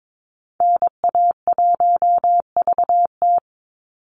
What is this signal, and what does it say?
Morse code, heard as a keyed tone, about 22 wpm, 700 Hz. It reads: NA1VT
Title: Call Signs Repeated 2x - 22wpm